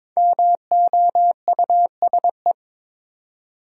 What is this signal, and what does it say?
Morse code, heard as a keyed tone, about 22 wpm, 690 Hz. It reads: MOUSE
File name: 5 Letter Words - First 3 Letters Given - 22wpm